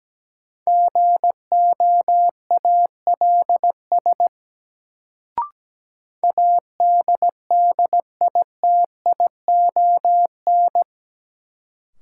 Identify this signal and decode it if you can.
Morse code, heard as a keyed tone, about 17 wpm, 700 Hz. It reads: GOALS ADDITION
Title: Sentences from Top 1000 Words - Review Part 6 - New Word Drill - 17wpm